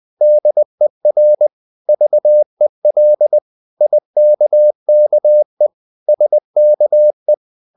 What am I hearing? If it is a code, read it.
Morse code, heard as a keyed tone, about 20 wpm, 600 Hz. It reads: DER VEL IKKE SKE
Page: Ferie 20 wpm